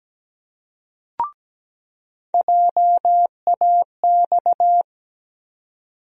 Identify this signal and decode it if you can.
Morse code, heard as a keyed tone, about 17 wpm, 700 Hz. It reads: JAX
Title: USA Call Sign Postfixes - 3 Letters 17wpm